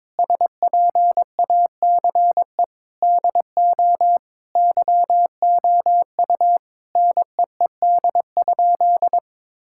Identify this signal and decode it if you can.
Morse code, heard as a keyed tone, about 22 wpm, 700 Hz. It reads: SPACE DO YOU NEED?
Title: Sentences from Top 1000 Words - Review Part 5 - 22wpm